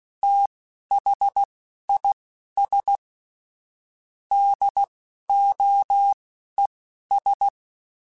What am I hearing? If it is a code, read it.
Morse code, heard as a keyed tone, about 16 wpm, spaced as for 11 wpm, 800 Hz. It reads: THIS DOES
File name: QSOs / QSOs-mp3s / WD0BGP 2@10